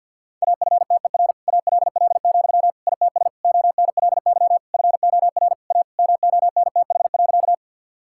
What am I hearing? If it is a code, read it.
Morse code, heard as a keyed tone, about 51 wpm, 700 Hz. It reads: AFTER ALL, ITS ONLY FOR A MONTH.
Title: Sentences from Top 1000 Words - Review Part 5 - 50wpm